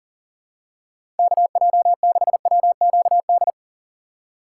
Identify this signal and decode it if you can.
Morse code, heard as a keyed tone, about 38 wpm, 700 Hz. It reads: KJ6WQD
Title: USA Call Signs - 6x Character Spacing - 40wpm